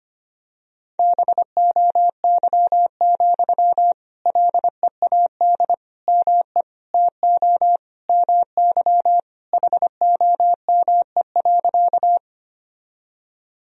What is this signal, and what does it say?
Morse code, heard as a keyed tone, about 25 wpm, 700 Hz. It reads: BOY, LEAD ME TO MY HOME.
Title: Sentences from Top 1000 Words - Review Part 2 - 25wpm